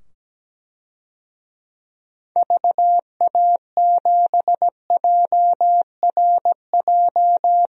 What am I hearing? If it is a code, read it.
Morse code, heard as a keyed tone, about 17 wpm, 700 Hz. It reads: VA7JRJ
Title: DXCC-100 Call Signs - 6 Characters - 17wpm